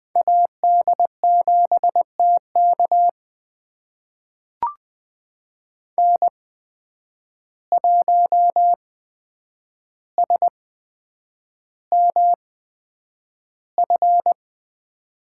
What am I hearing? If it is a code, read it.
Morse code, heard as a keyed tone, about 20 wpm, spaced as for 9 wpm, 700 Hz. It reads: AD7TK N1SMF
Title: USA Call Signs - 5 Characters - 8x Character Spacing - 20wpm